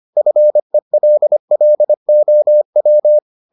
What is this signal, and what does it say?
Morse code, heard as a keyed tone, about 25 wpm, 600 Hz. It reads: FELLOW